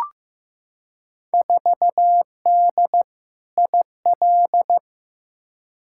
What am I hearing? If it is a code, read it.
Morse code, heard as a keyed tone, about 15 wpm, 700 Hz. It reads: 4D IL
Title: Field Day Exchanges Encore - 15wpm